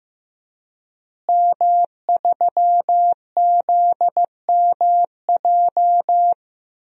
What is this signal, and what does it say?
Morse code, heard as a keyed tone, about 15 wpm, 700 Hz. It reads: M3ZMJ
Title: DXCC-100 Call Signs - 5 Characters - Repeated 3x - 15wpm